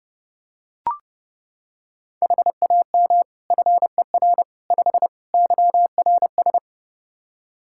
Morse code, heard as a keyed tone, about 30 wpm, 700 Hz. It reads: HAM FER 5 YRS
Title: Practice QSO - Exchange 3 - More Info - 30wpm